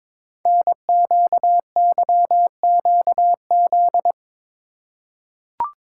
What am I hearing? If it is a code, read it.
Morse code, heard as a keyed tone, about 22 wpm, 700 Hz. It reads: NQYQZ
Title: Code Groups - 5 Letters - 22wpm